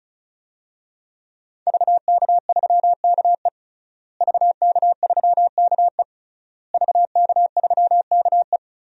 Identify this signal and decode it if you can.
Morse code, heard as a keyed tone, about 34 wpm, 700 Hz. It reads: VK3KE VK3KE VK3KE
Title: DXCC-100 Call Signs - 5 Characters - Repeated 3x - 35wpm